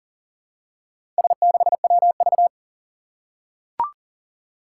Morse code, heard as a keyed tone, about 39 wpm, 700 Hz. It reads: S6WV
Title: Four Challenging Letter-Number Combinations 40wpm